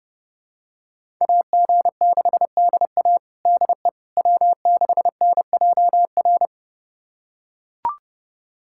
Morse code, heard as a keyed tone, about 30 wpm, 700 Hz. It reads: AG6DA DE W6NJR
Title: USA Callsign DE Callsign - 5 Characters - 30wpm